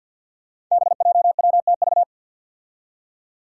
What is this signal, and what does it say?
Morse code, heard as a keyed tone, about 47 wpm, 700 Hz. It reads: BJWTV
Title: Five Challenging Letter-Number Combinations 50wpm